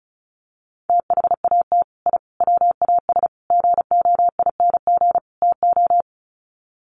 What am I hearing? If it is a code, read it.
Morse code, heard as a keyed tone, about 34 wpm, 700 Hz. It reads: THAT I WAS GOING TO
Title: Sets of 5 Words Encore 35wpm